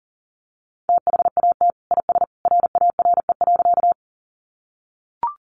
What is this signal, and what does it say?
Morse code, heard as a keyed tone, about 40 wpm, 700 Hz. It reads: THAT IS RARE.
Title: Sentences from Top 1100 Words 40wpm